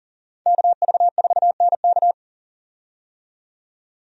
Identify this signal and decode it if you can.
Morse code, heard as a keyed tone, about 40 wpm, 700 Hz. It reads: KV4NK
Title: USA Call Signs - 5 Characters - 40wpm